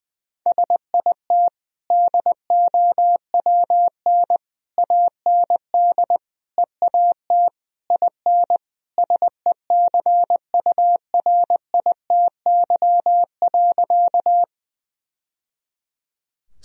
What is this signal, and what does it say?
Morse code, heard as a keyed tone, about 21 wpm, 700 Hz. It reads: SIT DOWN AND EAT IN SECURITY.
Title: Sentences from Top 1000 Words - Review Part 4 - 20wpm